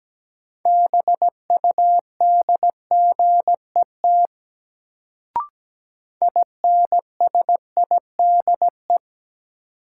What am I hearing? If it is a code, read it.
Morse code, heard as a keyed tone, about 17 wpm, 700 Hz. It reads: BUDGET INSIDE
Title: Sentences from Top 1200 Words - New Word Drill Alt - 17wpm